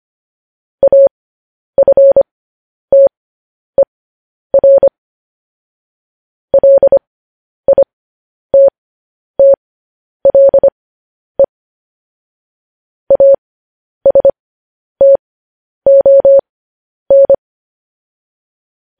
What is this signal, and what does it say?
Morse code, heard as a keyed tone, about 25 wpm, spaced as for 10 wpm, 550 Hz. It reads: AFTER LITTLE ASTON
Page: News Headlines in Morse Code at 10 WPM